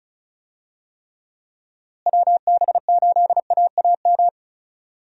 Morse code, heard as a keyed tone, about 36 wpm, 700 Hz. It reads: WB8AAM
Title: USA Call Signs - 6 Characters - Repeated 2x - 35wpm